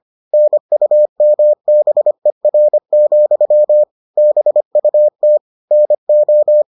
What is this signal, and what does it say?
Morse code, heard as a keyed tone, about 25 wpm, 600 Hz. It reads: NUMBER, BUT NO